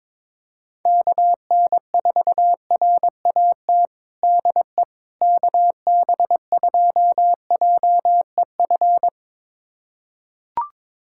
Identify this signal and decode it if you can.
Morse code, heard as a keyed tone, about 21 wpm, 700 Hz. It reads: KN4RAT DE KB2JEF
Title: USA Callsign DE Callsign - 6 Characters - 22wpm